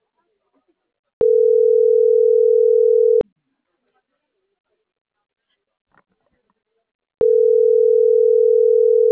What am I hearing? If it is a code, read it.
Ringback tone (North America), 440 + 480 Hz.